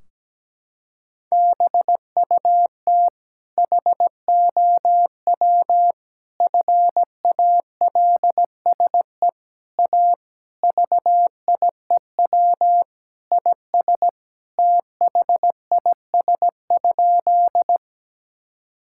Morse code, heard as a keyed tone, about 17 wpm, 700 Hz. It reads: BUT HOW FALSE A VIEW IS THIS?
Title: Sentences from Top 1600 Words 17wpm